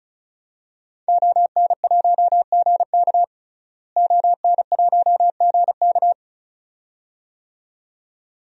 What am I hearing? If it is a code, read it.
Morse code, heard as a keyed tone, about 36 wpm, 700 Hz. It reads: ON1GK ON1GK
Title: DXCC-100 Call Signs - 5 Characters - Repeated 2x - 35wpm